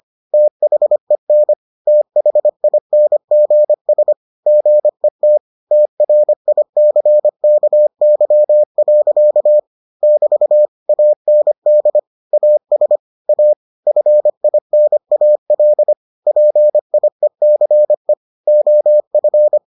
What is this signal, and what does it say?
Morse code, heard as a keyed tone, about 25 wpm, 600 Hz. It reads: THEN THINGS GET TRICKY. <BT> AND AS A FINAL PIECE OF